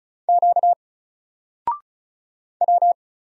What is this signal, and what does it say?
Morse code, heard as a keyed tone, about 36 wpm, 700 Hz. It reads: Q W
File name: Single-Letter-Number-35wpm